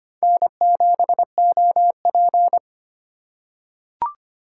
Morse code, heard as a keyed tone, about 25 wpm, 700 Hz. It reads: N7OP
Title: USA Call Signs - 4 Characters - 3x Character Spacing - 25wpm